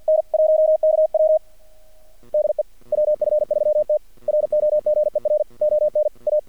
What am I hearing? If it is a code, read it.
Morse code, heard as a keyed tone, about 48 wpm, 620 Hz. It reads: N 1KW DE WF3T ROGER ON A
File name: archived / CRSnet